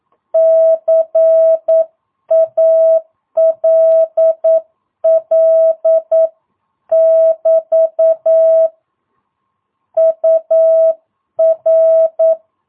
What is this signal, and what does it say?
Morse code, heard as a keyed tone, about 9 wpm, 650 Hz. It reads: CALL <BT> UR
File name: jh8cbh / 2022cw